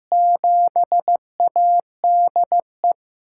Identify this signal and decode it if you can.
Morse code, heard as a keyed tone, about 15 wpm, 700 Hz. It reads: 7ADE
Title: USA Call Signs - 6 Characters - 6x Character Spacing - 15wpm